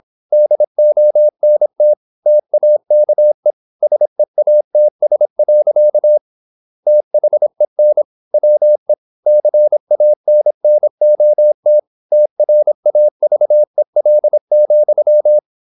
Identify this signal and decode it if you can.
Morse code, heard as a keyed tone, about 26 wpm, 600 Hz. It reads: DONT TAKE SEATS. THEN WE CANNOT TRAVEL,
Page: Never 26wpm | CW med Gnister